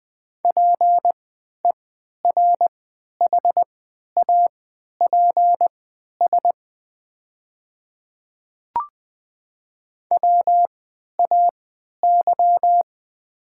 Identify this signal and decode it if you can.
Morse code, heard as a keyed tone, about 20 wpm, spaced as for 11 wpm, 700 Hz. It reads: PERHAPS WAY
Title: 3x Character Spacing - Top 500 Words - 20wpm